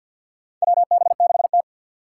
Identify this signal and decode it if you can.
Morse code, heard as a keyed tone, about 53 wpm, 700 Hz. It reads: WB6T
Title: USA Call Signs - 4 Characters - 3x Character Spacing - 50wpm